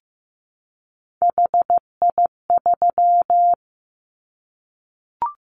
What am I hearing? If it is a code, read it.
Morse code, heard as a keyed tone, about 15 wpm, 700 Hz. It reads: HI3
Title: Callsign Prefixes for Top 100 DXCC - 15wpm